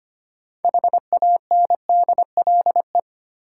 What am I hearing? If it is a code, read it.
Morse code, heard as a keyed tone, about 25 wpm, 700 Hz. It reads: HANDLE
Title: Sentences from Top 1100 Words - New Word Drill - 25wpm